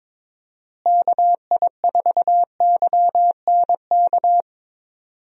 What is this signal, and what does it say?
Morse code, heard as a keyed tone, about 22 wpm, 700 Hz. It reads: KI4YNK
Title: USA Call Signs - 6 Characters - Repeated 3x - 22wpm